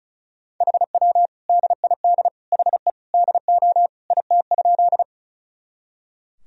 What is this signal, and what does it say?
Morse code, heard as a keyed tone, about 36 wpm, 700 Hz. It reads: HW DID HE DO IT?
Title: Sentences from Top 100 Words and Abbreviations 35wpm